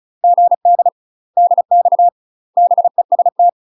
Morse code, heard as a keyed tone, about 36 wpm, 700 Hz. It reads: GD DX BEST
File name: UK MASTER QSOs TRAINING / 35WPM / UK MASTER QSOs 0029